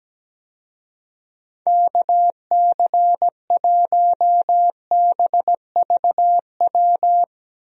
Morse code, heard as a keyed tone, about 17 wpm, 700 Hz. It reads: KC1BVW
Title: USA Call Signs - 6 Characters - 17wpm